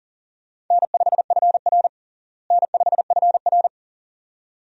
Morse code, heard as a keyed tone, about 40 wpm, 700 Hz. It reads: N5FR N5FR
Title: USA Call Signs - 4 Characters - Repeated 2x - 40wpm